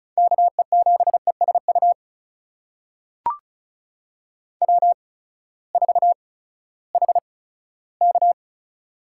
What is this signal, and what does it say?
Morse code, heard as a keyed tone, about 36 wpm, spaced as for 18 wpm, 700 Hz. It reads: KE7ESU W4HK
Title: USA Call Signs - 8x Character Spacing - 35wpm